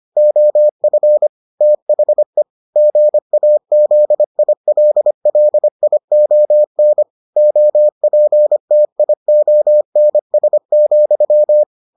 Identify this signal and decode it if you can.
Morse code, heard as a keyed tone, about 25 wpm, 600 Hz. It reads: OF THE GAZILLION OPTIONS,